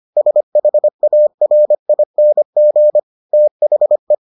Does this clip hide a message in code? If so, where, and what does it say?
Yes: Morse code from 0.2 to 4.1 s, heard as a keyed tone, about 25 wpm, 600 Hz. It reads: SHARING THE